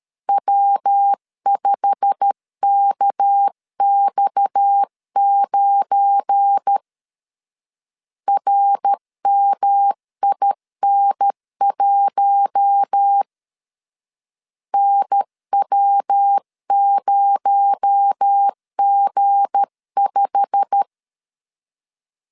Morse code, heard as a keyed tone, about 13 wpm, 800 Hz. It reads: W5KX9 RMIN1 NW0G5
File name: LEZIONI / 8 YJ45 / 8 YJ45 12wpm 2151